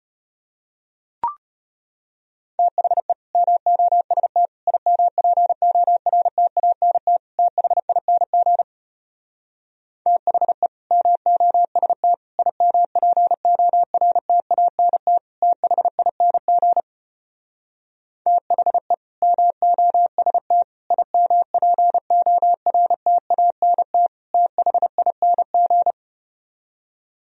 Morse code, heard as a keyed tone, about 33 wpm, 700 Hz. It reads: THE MOST IMPORTANT THING THE MOST IMPORTANT THING THE MOST IMPORTANT THING